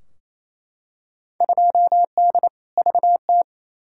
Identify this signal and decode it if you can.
Morse code, heard as a keyed tone, about 28 wpm, 700 Hz. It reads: 2D VT
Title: Field Day Exchanges Encore - 28wpm